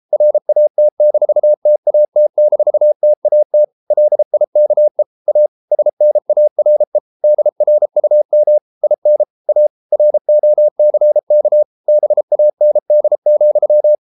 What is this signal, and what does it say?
Morse code, heard as a keyed tone, about 33 wpm, 600 Hz. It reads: RAT-TAT-TAT LIKE A SNARE DRUM IN A ROCK BAND,